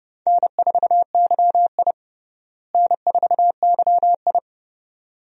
Morse code, heard as a keyed tone, about 30 wpm, 700 Hz. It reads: N4YI N4YI
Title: USA Call Signs - 4 Characters - Repeated 2x - 30wpm